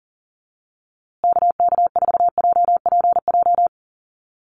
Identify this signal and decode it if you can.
Morse code, heard as a keyed tone, about 40 wpm, 700 Hz. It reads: KK4JPJ
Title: USA Call Signs Repeated 2x 40wpm